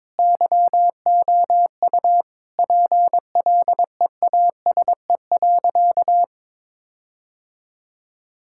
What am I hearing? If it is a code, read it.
Morse code, heard as a keyed tone, about 22 wpm, 700 Hz. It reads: YOU PLEASE.
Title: Sentences from Top 1600 Words 22wpm